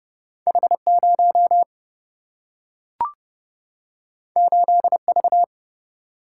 Morse code, heard as a keyed tone, about 30 wpm, 700 Hz. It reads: H0 8V
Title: Single-Letter Single-Number 30wpm